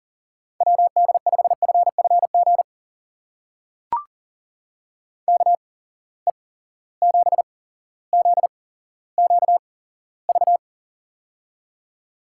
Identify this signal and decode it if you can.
Morse code, heard as a keyed tone, about 40 wpm, spaced as for 17 wpm, 700 Hz. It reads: WD5FFG KE7ZQV
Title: USA Call Signs - 6 Characters - 8x Character Spacing - 40wpm